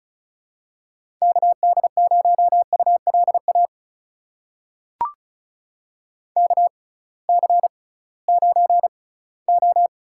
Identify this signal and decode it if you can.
Morse code, heard as a keyed tone, about 36 wpm, spaced as for 20 wpm, 700 Hz. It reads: KD0ULA KC9O
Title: USA Call Signs - 6 Characters - 6x Character Spacing - 35wpm